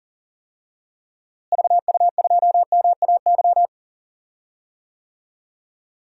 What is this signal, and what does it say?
Morse code, heard as a keyed tone, about 39 wpm, 700 Hz. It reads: VU2MAY
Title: DXCC-100 Call Signs - 6 Characters - Repeated 2x - 40wpm